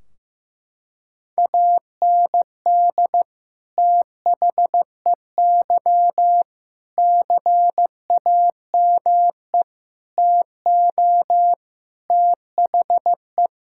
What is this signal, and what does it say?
Morse code, heard as a keyed tone, about 15 wpm, 700 Hz. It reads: AND THEY CAME TO THE
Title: Sentences from Top 1600 Words 15wpm